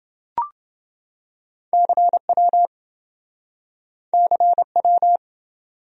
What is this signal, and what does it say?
Morse code, heard as a keyed tone, about 26 wpm, 700 Hz. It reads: CW CW